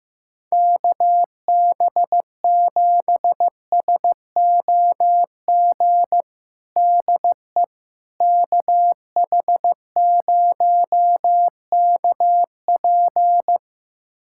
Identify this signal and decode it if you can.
Morse code, heard as a keyed tone, about 15 wpm, 700 Hz. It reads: KB7SOG DE KH0KP
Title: USA Callsign DE Callsign 15wpm